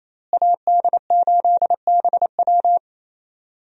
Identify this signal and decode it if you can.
Morse code, heard as a keyed tone, about 28 wpm, 700 Hz. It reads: AD8BW
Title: USA Call Signs - 5 Characters - Repeated 2x - 28wpm